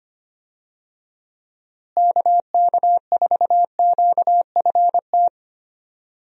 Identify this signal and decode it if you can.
Morse code, heard as a keyed tone, about 25 wpm, 700 Hz. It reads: KK4QFT
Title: USA Call Signs - 6 Characters - Repeated 3x - 25wpm